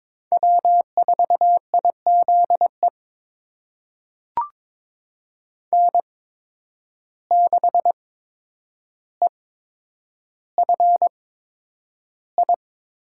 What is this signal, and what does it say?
Morse code, heard as a keyed tone, about 21 wpm, spaced as for 9 wpm, 700 Hz. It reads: W4IZE N6EFI
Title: USA Call Signs - 5 Characters - 8x Character Spacing - 22wpm